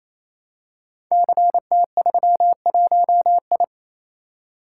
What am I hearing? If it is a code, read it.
Morse code, heard as a keyed tone, about 27 wpm, 700 Hz. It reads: CT31I
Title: Code Groups - 5 Letter:Numbers - 28wpm